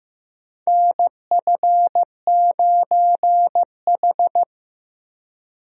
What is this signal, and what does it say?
Morse code, heard as a keyed tone, about 15 wpm, 700 Hz. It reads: NF9H
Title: USA Call Signs - 4 Characters - Repeated 2x - 15wpm